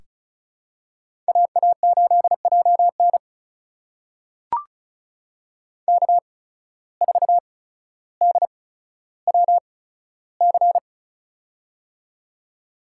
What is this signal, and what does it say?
Morse code, heard as a keyed tone, about 36 wpm, spaced as for 15 wpm, 700 Hz. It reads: AA8JN K4DWC
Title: USA Call Signs - 5 Characters - 8x Character Spacing - 35wpm